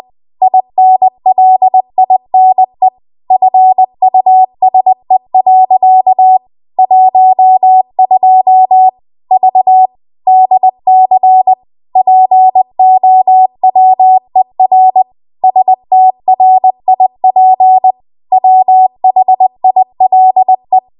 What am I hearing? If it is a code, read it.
Morse code, heard as a keyed tone, about 20 wpm, 750 Hz. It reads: INLINE FUSE. 12 V DC POWER STRIP WHILE
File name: Archive / 20 WPM / 20WPM